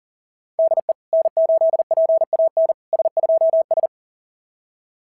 Morse code, heard as a keyed tone, about 40 wpm, 690 Hz. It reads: DE N8PAN S2S
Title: SOTA QSO Practice 40wpm